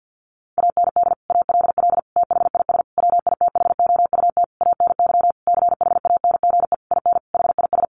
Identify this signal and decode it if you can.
Morse code, heard as a keyed tone, about 50 wpm, 700 Hz. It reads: AND ALL THIS WITHOUT ANY CHANGE IN HIS
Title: Sentences from Top 1200 Words 50wpm